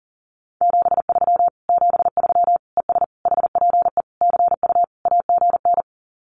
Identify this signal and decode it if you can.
Morse code, heard as a keyed tone, about 40 wpm, 700 Hz. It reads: 73 73 ES HPE CU AGN
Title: Common QSO Phrases Encore 40wpm